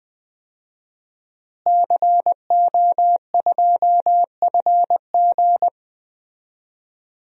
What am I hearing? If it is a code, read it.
Morse code, heard as a keyed tone, about 21 wpm, 700 Hz. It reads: CO2FG
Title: DXCC-100 Call Signs - 5 Characters - Repeated 2x - 20wpm